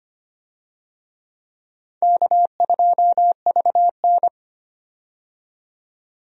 Morse code, heard as a keyed tone, about 25 wpm, 700 Hz. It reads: K2VN
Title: USA Call Signs - 4 Characters - 25wpm